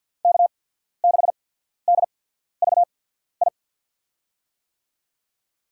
Morse code, heard as a keyed tone, about 54 wpm, spaced as for 14 wpm, 700 Hz. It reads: K6DVI
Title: USA Call Signs - 5 Characters - 8x Character Spacing - 50wpm